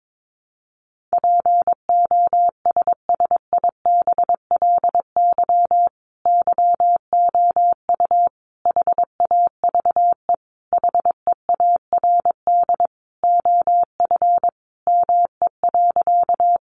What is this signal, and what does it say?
Morse code, heard as a keyed tone, about 22 wpm, 700 Hz. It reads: POSSIBLY YOU HAVE HEARD OF ME.
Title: Sentences from Top 1300 Words 22wpm